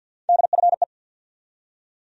Morse code, heard as a keyed tone, about 53 wpm, 700 Hz. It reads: DFE
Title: USA Call Sign Postfixes - 3 Letters 50wpm